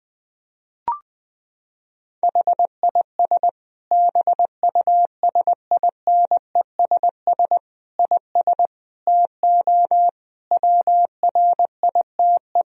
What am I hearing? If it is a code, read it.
Morse code, heard as a keyed tone, about 20 wpm, 700 Hz. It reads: HIS BUSINESS IS TO WRITE
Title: Sentences from Top 1000 Words - Review Part 3 - 20wpm